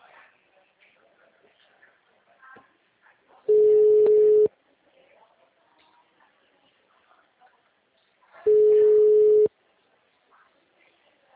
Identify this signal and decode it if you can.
Ringback tone (Europe, 425 Hz), with speech.